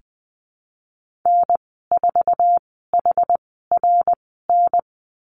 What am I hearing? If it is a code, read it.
Morse code, heard as a keyed tone, about 20 wpm, spaced as for 15 wpm, 700 Hz. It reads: N4HRN
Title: 076 Lesson 13g - Letter L - Call Signs 20wpm